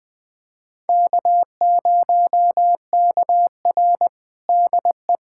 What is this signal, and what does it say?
Morse code, heard as a keyed tone, about 21 wpm, 700 Hz. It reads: K0KR DE
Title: USA Callsign DE Callsign - 4 Characters - 20wpm